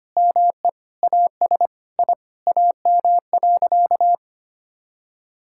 Morse code, heard as a keyed tone, about 25 wpm, 700 Hz. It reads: ME AS I AM.
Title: Sentences from Top 1000 Words - Review Part 6 - 25wpm